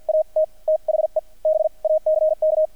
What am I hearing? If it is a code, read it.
Morse code, heard as a keyed tone, about 50 wpm, spaced as for 40 wpm, 620 Hz. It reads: AT THE BACK
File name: archived / CRSnet